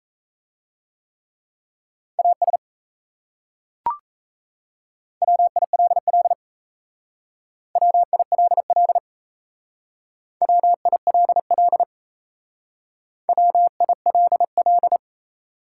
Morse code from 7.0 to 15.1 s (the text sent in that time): WILL WILL WILL